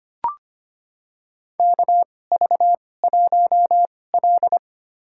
Morse code, heard as a keyed tone, about 25 wpm, spaced as for 18 wpm, 700 Hz. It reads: KV1L
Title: USA Call Signs - 2x Character Spacing - 25wpm